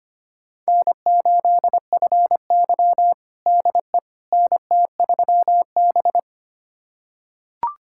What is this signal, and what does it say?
Morse code, heard as a keyed tone, about 25 wpm, 700 Hz. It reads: N8FY DE NT3B
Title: USA Callsign DE Callsign - 4 Characters - 25wpm